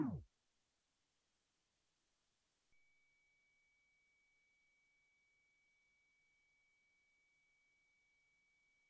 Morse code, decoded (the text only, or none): none